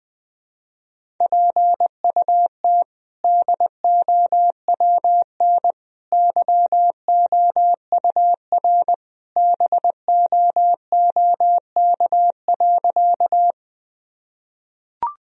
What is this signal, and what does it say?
Morse code, heard as a keyed tone, about 20 wpm, 700 Hz. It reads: PUT DOWN YOUR BOOK.
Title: Sentences from Top 1000 Words - Review Part 1 - 20wpm